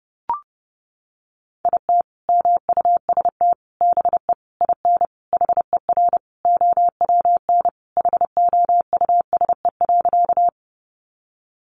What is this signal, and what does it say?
Morse code, heard as a keyed tone, about 30 wpm, 700 Hz. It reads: IT MUST BE IN HER OWN HOUSE.
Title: Sentences from Top 300 Words 30wpm